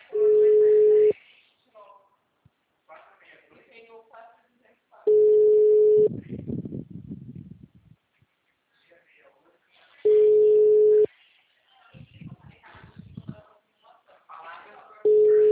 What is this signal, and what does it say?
Ringback tone (Europe, 425 Hz), with speech.